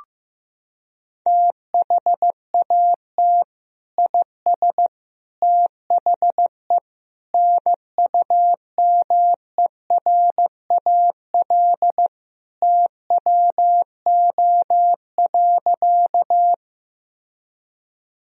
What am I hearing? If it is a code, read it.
Morse code, heard as a keyed tone, about 15 wpm, 700 Hz. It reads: THAT IS THE NUMERAL TWO.
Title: Sentences from Top 1000 Words - Review Part 2 - 15wpm